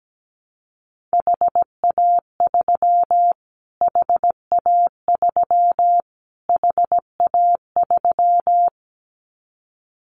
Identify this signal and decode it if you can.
Morse code, heard as a keyed tone, about 17 wpm, 700 Hz. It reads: HA3 HA3 HA3
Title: Callsign Prefixes for Top 25 DXCC - Repeated 3x - 17wpm